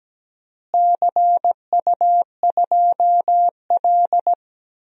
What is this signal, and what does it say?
Morse code, heard as a keyed tone, about 17 wpm, 700 Hz. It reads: CU2L
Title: DXCC-100 Call Signs - 4 Characters - Repeated 3x - 17wpm